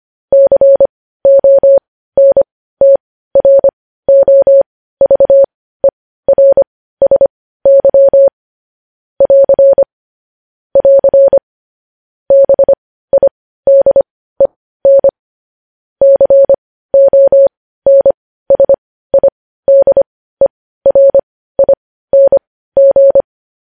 Morse code, heard as a keyed tone, about 25 wpm, spaced as for 15 wpm, 550 Hz. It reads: CONTROVERSY <AR> <AR> BIDEN CONSIDERING
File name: podcast / News headlines-1x-25x15